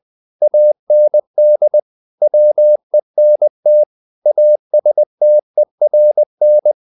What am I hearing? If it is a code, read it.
Morse code, heard as a keyed tone, about 20 wpm, 600 Hz. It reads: AND WENT ASTERN
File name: the old man and the sea part 2 20wpm